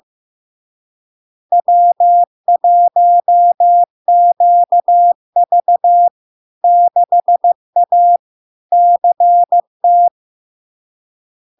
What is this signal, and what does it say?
Morse code, heard as a keyed tone, about 15 wpm, 700 Hz. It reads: W1QV 6A CT
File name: LARK-FD-2022-Log-in-CW-15wpm-700-Hz